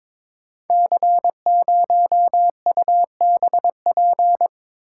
Morse code, heard as a keyed tone, about 22 wpm, 700 Hz. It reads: C0UBP
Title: Code Groups - 5 Letter:Numbers - 22wpm